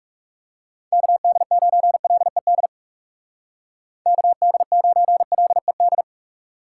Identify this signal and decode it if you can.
Morse code, heard as a keyed tone, about 37 wpm, 700 Hz. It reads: KD9LED KD9LED
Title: Speed Racing - USA Call Signs 30wpm